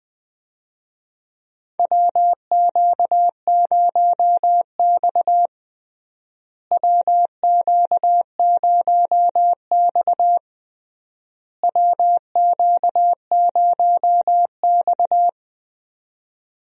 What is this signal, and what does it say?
Morse code, heard as a keyed tone, about 20 wpm, 700 Hz. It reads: WQ0X WQ0X WQ0X
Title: USA Call Signs - 4 Characters - Repeated 3x - 20wpm